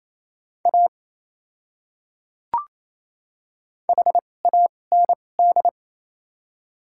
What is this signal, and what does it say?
Morse code, heard as a keyed tone, about 28 wpm, spaced as for 20 wpm, 700 Hz. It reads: A HAND
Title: 2x Character Spacing - Top 500 Words - 28wpm